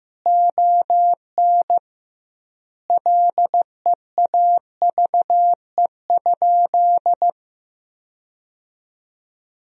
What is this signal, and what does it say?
Morse code, heard as a keyed tone, about 16 wpm, 700 Hz. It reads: ON LEAVE?
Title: 2x Word-Spacing - Sentences from Top 400 Words - 15wpm